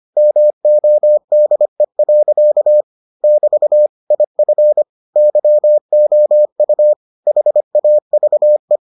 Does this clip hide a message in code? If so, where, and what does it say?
Yes: Morse code from 0.2 to 8.8 s, heard as a keyed tone, about 25 wpm, 600 Hz. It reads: MODE. <BT> IF YOU HAVE